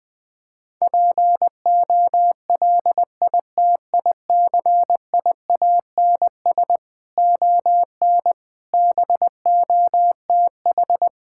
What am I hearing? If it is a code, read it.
Morse code, heard as a keyed tone, about 20 wpm, 700 Hz. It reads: POLITICIANS ON BOTH
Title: Sentences with 6-Word Phrases - Spoken Beginning - 20wpm